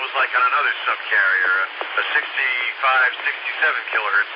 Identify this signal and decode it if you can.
Emergency Alert System (SAME) tones, heard as harsh buzzing bursts, with speech. It is heard from 1.1 to 3.9 s.